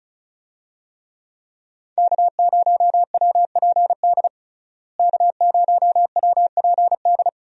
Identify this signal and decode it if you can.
Morse code, heard as a keyed tone, about 34 wpm, 700 Hz. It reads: K0WPD K0WPD
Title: USA Call Signs - 5 Characters - Repeated 2x - 35wpm